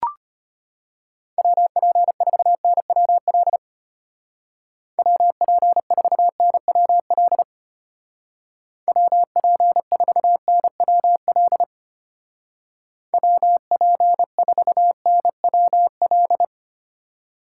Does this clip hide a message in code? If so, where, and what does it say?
Yes: Morse code from 1.4 to 16.4 s, heard as a keyed tone, about 33 wpm, 700 Hz. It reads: WP4NWL WP4NWL WP4NWL WP4NWL